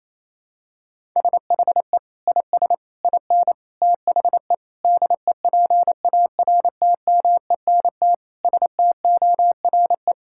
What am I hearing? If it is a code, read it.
Morse code, heard as a keyed tone, about 28 wpm, 700 Hz. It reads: SHE IS IN THE DEPARTMENT STORE
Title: Sentences from Top 1000 Words - Review Part 5 - 28wpm